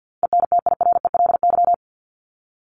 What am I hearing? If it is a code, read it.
Morse code, heard as a keyed tone, about 50 wpm, 700 Hz. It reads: ENTIRELY
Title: LY Suffix - Top 100 Words with Base Words 50wpm